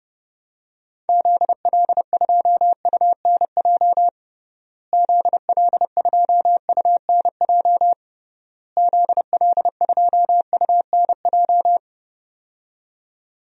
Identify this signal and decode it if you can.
Morse code, heard as a keyed tone, about 32 wpm, 700 Hz. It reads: ZL2UNJ ZL2UNJ ZL2UNJ
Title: DXCC-100 Call Signs - 6 Characters - Repeated 3x - 30wpm